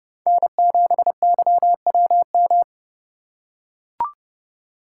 Morse code, heard as a keyed tone, about 32 wpm, 700 Hz. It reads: N7YWM
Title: USA Call Signs - 3x Character Spacing - 30wpm